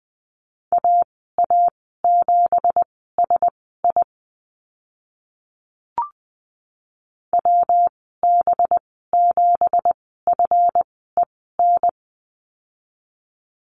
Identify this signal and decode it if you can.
Morse code, heard as a keyed tone, about 20 wpm, spaced as for 15 wpm, 700 Hz. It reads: AA7SI WB7FEN
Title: 186 Lesson 30g - Letter b - Call Signs 20wpm